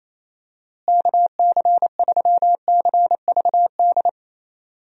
Morse code, heard as a keyed tone, about 27 wpm, 700 Hz. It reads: KC3CVD
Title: USA Call Signs - 6 Characters - Repeated 3x - 28wpm